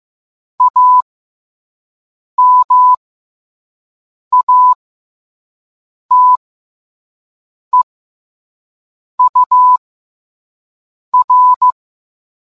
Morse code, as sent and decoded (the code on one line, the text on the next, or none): .- -- .- - . ..- .-.
AMATEUR